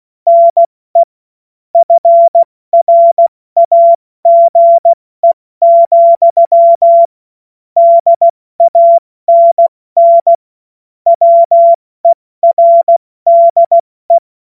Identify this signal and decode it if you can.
Morse code, heard as a keyed tone, about 16 wpm, 680 Hz. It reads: NE FRAGE, DANN WERDE
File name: Morsekurs / Witze / Witz 09 Januar 2024 70 F80